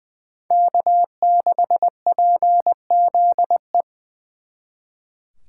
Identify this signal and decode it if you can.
Morse code, heard as a keyed tone, about 23 wpm, 700 Hz. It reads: K6PZE
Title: Speed Racing - USA 1x3 Call Signs 20wpm